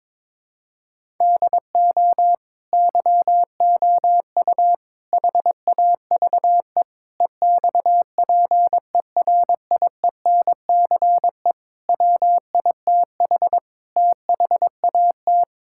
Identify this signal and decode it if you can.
Morse code, heard as a keyed tone, about 22 wpm, 700 Hz. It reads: DO YOU HAVE EXPERIENCE WITH THAT
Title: Sentences from Top 1000 Words - Review Part 5 - 22wpm